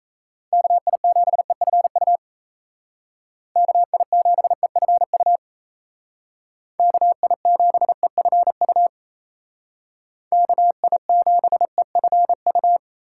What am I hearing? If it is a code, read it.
Morse code, heard as a keyed tone, about 35 wpm, 700 Hz. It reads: KI7EFU KI7EFU KI7EFU KI7EFU
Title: Speed Racing - USA 2x3 Call Signs 28wpm